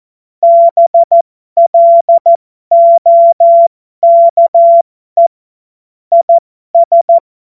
Morse code, heard as a keyed tone, about 14 wpm, spaced as for 12 wpm, 680 Hz. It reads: BLOKE IS
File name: Morsekurs / Witze